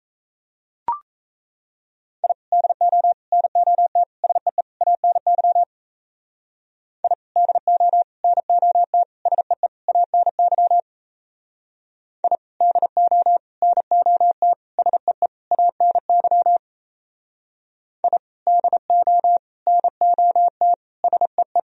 Morse code, heard as a keyed tone, about 35 wpm, 700 Hz. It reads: I DO NOT SEE ANY I DO NOT SEE ANY I DO NOT SEE ANY I DO NOT SEE
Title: Speed-Racing - Sets of 5 Words 28wpm